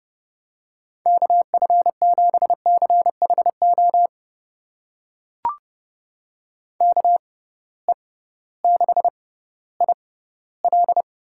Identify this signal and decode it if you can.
Morse code, heard as a keyed tone, about 30 wpm, spaced as for 16 wpm, 700 Hz. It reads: KF7CHO KE6IL
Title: USA Call Signs - 6 Characters - 6x Character Spacing - 30wpm